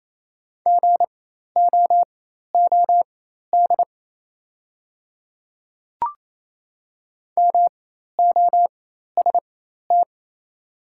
Morse code, heard as a keyed tone, about 28 wpm, spaced as for 13 wpm, 690 Hz. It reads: GOOD MOST
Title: 4x Character Spacing - Top 200 Words - 28wpm